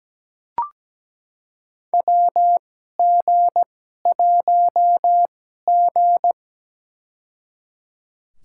Morse code, heard as a keyed tone, about 17 wpm, spaced as for 12 wpm, 700 Hz. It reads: WG1G
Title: USA Call Signs - 4 Characters - 2x Character Spacing - 17wpm